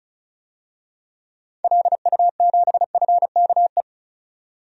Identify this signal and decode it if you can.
Morse code, heard as a keyed tone, about 36 wpm, 700 Hz. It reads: LU7FKE
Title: DXCC-100 Call Signs - 6 Characters - 35wpm